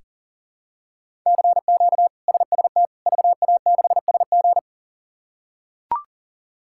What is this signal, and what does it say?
Morse code, heard as a keyed tone, about 37 wpm, 700 Hz. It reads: CQ SST VA6SG
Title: K1USN SST QSO Practice - 40wpm